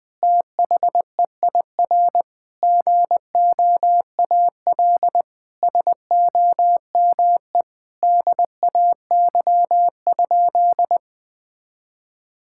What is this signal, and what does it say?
Morse code, heard as a keyed tone, about 20 wpm, 700 Hz. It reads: THEIR GOAL SOME DAY?
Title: Sentences from Top 1000 Words - Review Part 6 - 20wpm